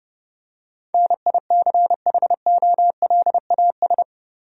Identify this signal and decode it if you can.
Morse code, heard as a keyed tone, about 32 wpm, 700 Hz. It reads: NICHOLAS
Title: USA Common Names - 30wpm